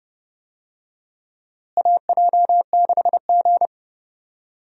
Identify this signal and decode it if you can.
Morse code, heard as a keyed tone, about 30 wpm, 700 Hz. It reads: AJ6G
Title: USA Call Signs - 4 Characters - Repeated 2x - 30wpm